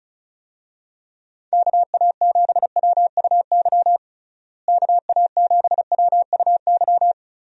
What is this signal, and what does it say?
Morse code, heard as a keyed tone, about 34 wpm, 700 Hz. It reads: KA7WUY KA7WUY
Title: USA Call Signs - 6 Characters - Repeated 2x - 35wpm